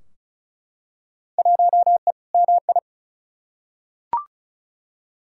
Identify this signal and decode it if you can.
Morse code, heard as a keyed tone, about 36 wpm, 700 Hz. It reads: 1E MI
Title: Field Day Exchanges - 35wpm